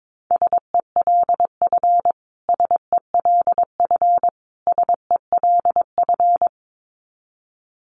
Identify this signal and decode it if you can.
Morse code, heard as a keyed tone, about 22 wpm, 700 Hz. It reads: SELF SELF SELF
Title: Top 500 Words Repeated 22wpm